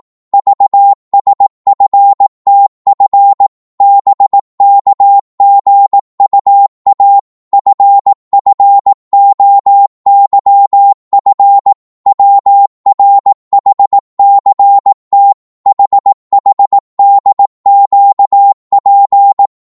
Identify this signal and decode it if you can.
Morse code, heard as a keyed tone, about 18 wpm, 800 Hz. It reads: VSFTF BKGUA FFOYF WRHCT HHDQP
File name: Morse / ltr-18wpm